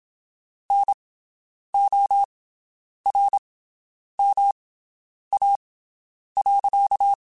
Morse code, heard as a keyed tone, about 27 wpm, spaced as for 9 wpm, 800 Hz. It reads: NORMA.
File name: QSOs Spaced